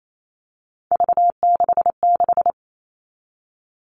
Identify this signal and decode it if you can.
Morse code, heard as a keyed tone, about 28 wpm, 700 Hz. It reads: V66
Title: All Cut Numbers - 3-Digit 28wpm